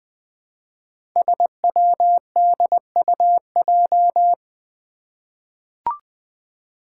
Morse code, heard as a keyed tone, about 21 wpm, 700 Hz. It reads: SWDUJ
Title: Five Challenging Letter-Number Combinations 20wpm